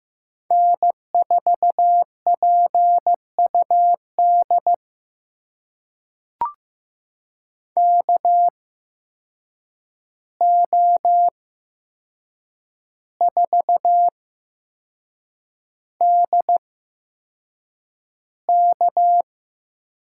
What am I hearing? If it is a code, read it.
Morse code, heard as a keyed tone, about 15 wpm, spaced as for 6 wpm, 700 Hz. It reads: N4PUD KO4DK
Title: USA Call Signs - 5 Characters - 8x Character Spacing - 15wpm